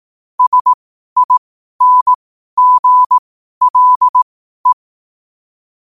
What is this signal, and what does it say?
Morse code, heard as a keyed tone, about 18 wpm, spaced as for 12 wpm, 1000 Hz. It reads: SINGLE
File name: qotdfiles / 12wpm